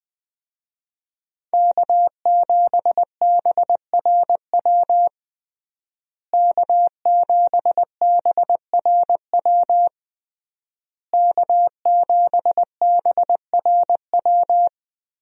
Morse code, heard as a keyed tone, about 20 wpm, 700 Hz. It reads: K7BRW K7BRW K7BRW
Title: USA Call Signs - 5 Characters - Repeated 3x - 20wpm